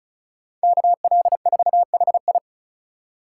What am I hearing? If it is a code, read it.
Morse code, heard as a keyed tone, about 34 wpm, 700 Hz. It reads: KL4HI
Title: USA Call Signs - 5 Characters - 3x Character Spacing - 35wpm